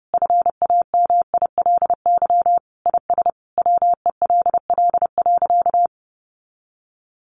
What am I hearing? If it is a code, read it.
Morse code, heard as a keyed tone, about 30 wpm, 700 Hz. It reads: FAMILY IS WELL.
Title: Sentences from Top 800 Words 30wpm